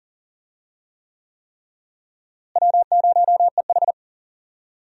Morse code, heard as a keyed tone, about 40 wpm, 700 Hz. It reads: W0EH
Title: USA Call Signs - 4 Characters - Repeated 3x - 40wpm